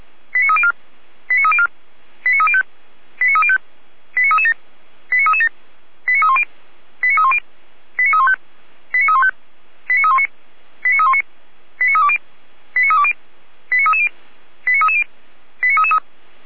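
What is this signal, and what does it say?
Five-tone selective call (ZVEI), heard as quick runs of five tones, tones of about 70 ms. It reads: 89304 89304 89305 89305 89337 89337 89310 89310 89315 89315 89319 89319 89320 89320 89330 89330 89333
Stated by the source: ZVEI five-tone call 89246 received by SDR.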